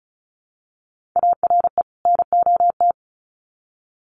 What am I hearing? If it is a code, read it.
Morse code, heard as a keyed tone, about 34 wpm, 700 Hz. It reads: ARE NOT
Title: Sets of 2 Words 35wpm